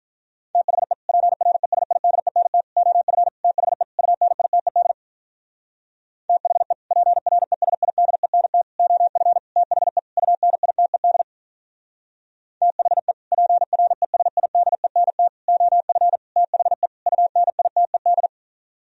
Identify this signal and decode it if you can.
Morse code, heard as a keyed tone, about 47 wpm, 700 Hz. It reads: THE PRESIDENT OF THE UNITED THE PRESIDENT OF THE UNITED THE PRESIDENT OF THE UNITED